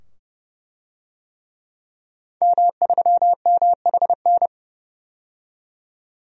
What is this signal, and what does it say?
Morse code, heard as a keyed tone, about 30 wpm, 700 Hz. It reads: M3MHN
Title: DXCC-100 Call Signs - 5 Characters - 30wpm